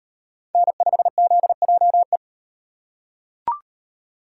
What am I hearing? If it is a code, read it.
Morse code, heard as a keyed tone, about 33 wpm, 700 Hz. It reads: N5ZJE